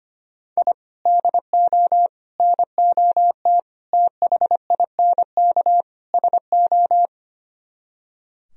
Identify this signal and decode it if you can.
Morse code, heard as a keyed tone, about 32 wpm, 700 Hz. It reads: I DO NOT THINK SO